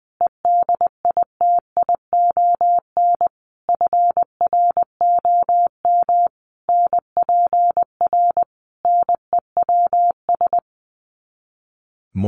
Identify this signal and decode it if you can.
Morse code, heard as a keyed tone, about 20 wpm, 700 Hz. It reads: EDITION FROM NPR NEWS
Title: Sets of 5 Words Encore 20wpm